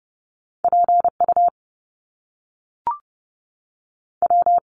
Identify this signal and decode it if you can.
Morse code, heard as a keyed tone, about 30 wpm, 700 Hz. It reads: P U W
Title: Top Two Letter Combinations in Words 30wpm